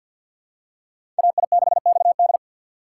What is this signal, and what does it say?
Morse code, heard as a keyed tone, about 52 wpm, 700 Hz. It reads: AI6XD
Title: USA Call Signs - 5 Characters - 6x Character Spacing - 50wpm